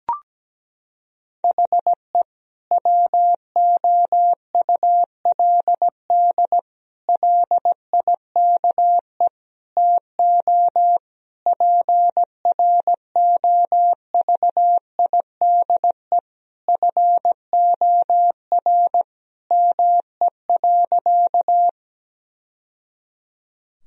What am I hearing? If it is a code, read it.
Morse code, heard as a keyed tone, about 17 wpm, 700 Hz. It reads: HE WOULD LIKE TO PROVIDE FOR ME.
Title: Sentences from Top 1000 Words - Review Part 3 - 17wpm